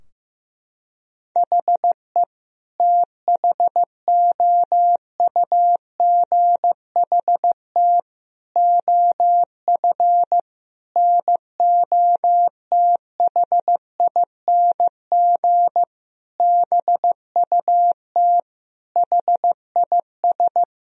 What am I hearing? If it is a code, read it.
Morse code, heard as a keyed tone, about 15 wpm, 700 Hz. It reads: HE THOUGHT OF NOTHING BUT HIS
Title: Sentences from Top 1000 Words - Review Part 4 - 15wpm